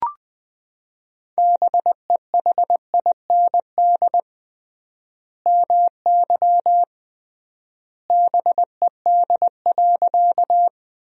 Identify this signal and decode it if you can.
Morse code, heard as a keyed tone, about 21 wpm, 700 Hz. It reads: BEHIND MY BED.
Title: 3x Word-Spacing - Sentences from Top 500 Words - 20wpm